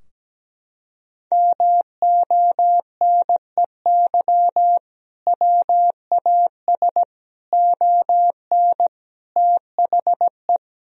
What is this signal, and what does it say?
Morse code, heard as a keyed tone, about 17 wpm, 700 Hz. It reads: MONEY WAS ON THE
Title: Sentences from Top 1000 Words - Review Part 2 - 17wpm